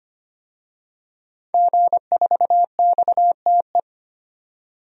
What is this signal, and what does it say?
Morse code, heard as a keyed tone, about 25 wpm, 700 Hz. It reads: G4XTE
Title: DXCC-100 Call Signs - 5 Characters - 25wpm